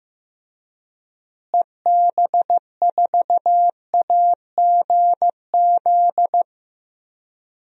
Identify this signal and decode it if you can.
Morse code, heard as a keyed tone, about 15 wpm, 700 Hz. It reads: EB4AGZ
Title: DXCC-100 Call Signs - 6 Characters - Repeated 3x - 15wpm